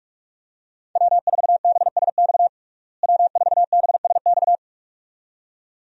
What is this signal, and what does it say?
Morse code, heard as a keyed tone, about 46 wpm, 700 Hz. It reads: W4BSX W4BSX
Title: USA Call Signs - 5 Characters - Repeated 2x - 45wpm